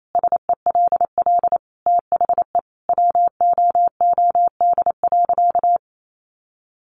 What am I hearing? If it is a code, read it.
Morse code, heard as a keyed tone, about 28 wpm, 700 Hz. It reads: SELL THE WOOD.
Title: Sentences from Top 1100 Words 28wpm